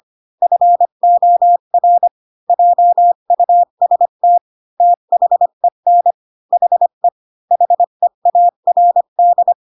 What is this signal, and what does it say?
Morse code, heard as a keyed tone, about 25 wpm, 700 Hz. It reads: FOR JUST THEN HE HEARD
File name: Pinocchio / Pinocchio ch4 20 25 ik1hge